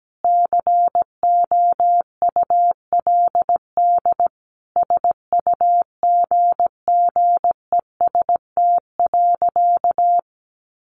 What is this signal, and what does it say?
Morse code, heard as a keyed tone, about 17 wpm, 700 Hz. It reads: COULD SUGGEST.
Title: Sentences from Top 1500 Words 17wpm